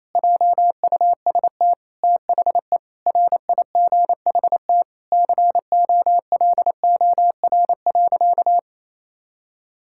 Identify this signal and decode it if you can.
Morse code, heard as a keyed tone, about 27 wpm, 700 Hz. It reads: JUST THE RIGHT COLOR.
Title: Sentences from Top 1000 Words - Review Part 6 - 28wpm